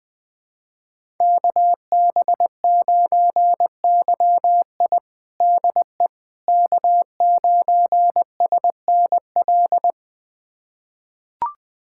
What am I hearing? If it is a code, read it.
Morse code, heard as a keyed tone, about 20 wpm, 700 Hz. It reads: KB9YI DE K9SNL
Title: USA Callsign DE Callsign - 5 Characters - 20wpm